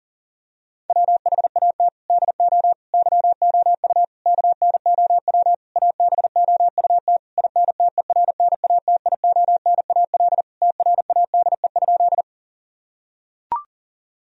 Morse code, heard as a keyed tone, about 40 wpm, 700 Hz. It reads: WHAT DO YOU KNOW ABOUT INTERNATIONAL TRADE?
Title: Sentences from Top 1000 Words - Review Part 4 - 40wpm